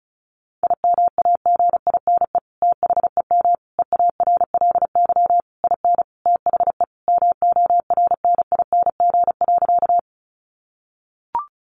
Morse code, heard as a keyed tone, about 34 wpm, 700 Hz. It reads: IMAGINE THEM EARLY IN THE MORNING.
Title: Sentences from Top 1400 Words 35wpm